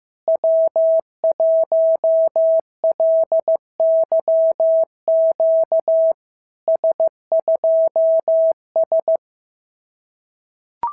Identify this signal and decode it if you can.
Morse code, heard as a keyed tone, about 15 wpm, 660 Hz. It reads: W1LYQ S2S
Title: SOTA QSO Practice 15wpm